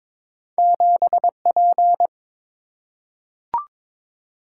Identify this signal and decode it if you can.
Morse code, heard as a keyed tone, about 22 wpm, 690 Hz. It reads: 7P
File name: practice / 22wpm / Two Letter-Number Combinations - 22wpm